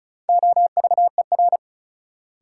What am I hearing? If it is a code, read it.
Morse code, heard as a keyed tone, about 34 wpm, 700 Hz. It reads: OVER
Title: Sentences from Top 100 Words - New Word Drill - 35wpm